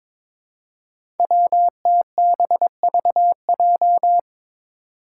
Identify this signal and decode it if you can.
Morse code, heard as a keyed tone, about 22 wpm, 700 Hz. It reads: WTBVJ
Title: Five Challenging Letter-Number Combinations 22wpm